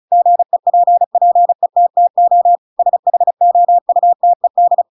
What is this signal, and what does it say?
Morse code, heard as a keyed tone, about 34 wpm, 700 Hz. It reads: GEPPETTO SHOUTED
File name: Pinocchio / Pinocchio ch7 35 35 ik1hge